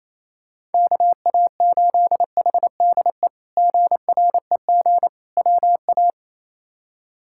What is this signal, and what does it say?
Morse code, heard as a keyed tone, about 28 wpm, 700 Hz. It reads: KA8HDE GREG WA
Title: K1USN SST QSO Practice - 28wpm